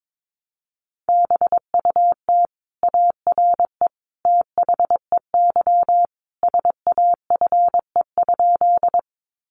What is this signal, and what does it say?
Morse code, heard as a keyed tone, about 22 wpm, 700 Hz. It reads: BUT ARE THEY SAFE?
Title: Sentences from Top 900 Words 22wpm